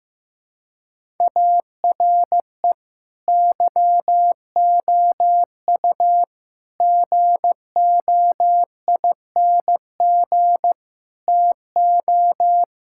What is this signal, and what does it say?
Morse code, heard as a keyed tone, about 15 wpm, 700 Hz. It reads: ARE YOU GOING TO
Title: Sentences from Top 1000 Words - Review Part 5 - 15wpm